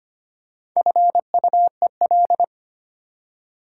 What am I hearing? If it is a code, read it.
Morse code, heard as a keyed tone, about 25 wpm, 700 Hz. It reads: FUEL
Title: Four Letter Words - 25wpm